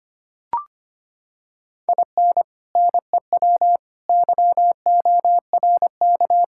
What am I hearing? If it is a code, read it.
Morse code, heard as a keyed tone, about 25 wpm, 700 Hz. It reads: IN NEW YORK
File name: practice / 25wpm / Sentences with 4-Word Phrases - 25wpm